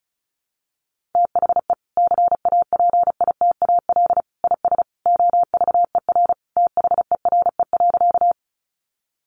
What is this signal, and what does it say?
Morse code, heard as a keyed tone, about 34 wpm, 700 Hz. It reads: THE CAPITAL IS OVER THERE.
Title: Sentences from Top 1000 Words 35wpm